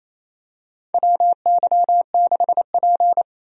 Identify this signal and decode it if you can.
Morse code, heard as a keyed tone, about 28 wpm, 700 Hz. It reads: WY6P
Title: USA Call Signs - 4 Characters - 28wpm